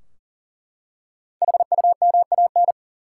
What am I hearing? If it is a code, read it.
Morse code, heard as a keyed tone, about 40 wpm, 700 Hz. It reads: HUMAN
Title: Five Letter Words - 40wpm